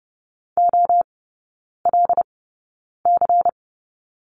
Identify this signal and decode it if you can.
Morse code, heard as a keyed tone, about 30 wpm, spaced as for 9 wpm, 700 Hz. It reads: OLC
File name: Lesson 16e - Letter c - Three Characters 30wpm